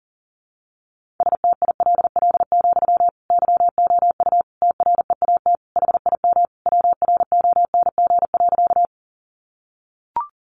Morse code, heard as a keyed tone, about 40 wpm, 700 Hz. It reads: STILL, YOU TREAT HIM WRONG.
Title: Sentences from Top 1300 Words 40wpm